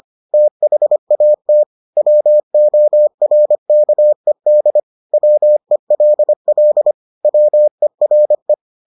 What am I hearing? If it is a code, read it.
Morse code, heard as a keyed tone, about 25 wpm, 600 Hz. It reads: THAT WORKED WELL WERE